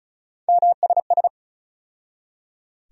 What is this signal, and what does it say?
Morse code, heard as a keyed tone, about 36 wpm, 700 Hz. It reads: MSS
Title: USA Call Sign Postfixes - 3 Letters 35wpm